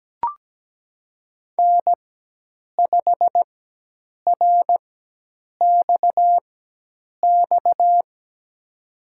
Morse code, heard as a keyed tone, about 17 wpm, spaced as for 8 wpm, 700 Hz. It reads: N5RXX
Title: USA Call Signs - 5 Characters - 4x Character Spacing - 17wpm